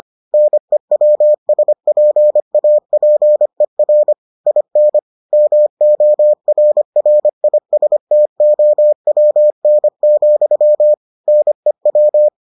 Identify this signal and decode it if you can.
Morse code, heard as a keyed tone, about 25 wpm, 600 Hz. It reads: NEWSPAPER IN MORRISTOWN, NEW